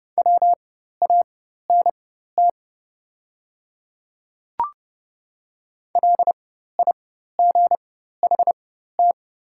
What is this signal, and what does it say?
Morse code, heard as a keyed tone, about 30 wpm, spaced as for 14 wpm, 700 Hz. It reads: WANT LIGHT
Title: 4x Character Spacing - Top 200 Words - 30wpm